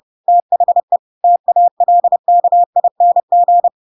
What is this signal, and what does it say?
Morse code, heard as a keyed tone, about 30 wpm, 700 Hz. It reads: THE TALKING
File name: Pinocchio / Pinocchio ch7 30 30 ik1hge